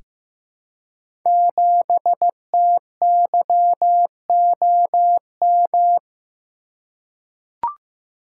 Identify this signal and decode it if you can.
Morse code, heard as a keyed tone, about 15 wpm, 700 Hz. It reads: 7TYOM
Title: Code Groups - 5 Letter:Numbers - 15wpm